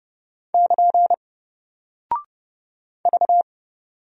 Morse code, heard as a keyed tone, about 30 wpm, 700 Hz. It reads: <KN> V
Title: Single Character (Letter:Number:Punctuation:Prosign) Alt 30wpm